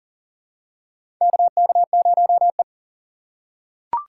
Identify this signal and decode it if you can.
Morse code, heard as a keyed tone, about 40 wpm, 700 Hz. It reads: KK0E
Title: USA Call Signs - 4 Characters - 3x Character Spacing - 40wpm